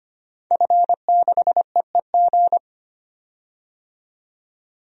Morse code, heard as a keyed tone, about 25 wpm, 700 Hz. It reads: F6EEG
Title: DXCC-100 Call Signs - 5 Characters - 25wpm